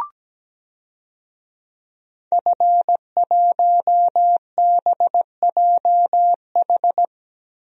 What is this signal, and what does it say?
Morse code, heard as a keyed tone, about 17 wpm, 700 Hz. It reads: F1BJH
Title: DXCC-100 Call Signs - 5 Characters - Repeated 2x - 17wpm